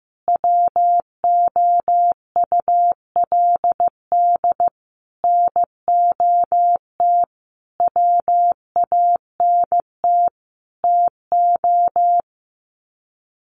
Morse code, heard as a keyed tone, about 15 wpm, 700 Hz. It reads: WOULD NOT WANT TO
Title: Sets of 5 Words 15wpm